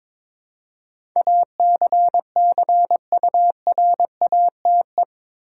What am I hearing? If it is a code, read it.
Morse code, heard as a keyed tone, about 22 wpm, 700 Hz. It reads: ACCURATE
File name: practice / 22wpm / Eight Letter Words - 22wpm